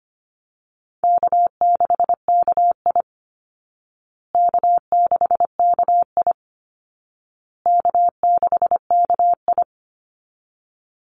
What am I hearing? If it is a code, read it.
Morse code, heard as a keyed tone, about 25 wpm, 700 Hz. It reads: K6KI K6KI K6KI
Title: Easy Call Signs Repeated 3x - 25wpm